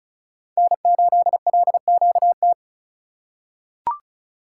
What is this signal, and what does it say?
Morse code, heard as a keyed tone, about 36 wpm, 700 Hz. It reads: N8LQT
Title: USA Call Signs - 5 Characters - 3x Character Spacing - 35wpm